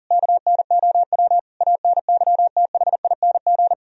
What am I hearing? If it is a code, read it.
Morse code, heard as a keyed tone, about 58 wpm, spaced as for 49 wpm, 700 Hz. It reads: KNOW ANYTHING
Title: Speed-Racing - Sets of 5 Words 40wpm